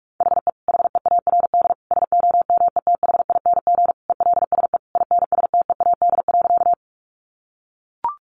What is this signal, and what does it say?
Morse code, heard as a keyed tone, about 44 wpm, 700 Hz. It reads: HE HEARD SOMETHING ELSE INSTEAD.
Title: Sentences from Top 800 Words 45wpm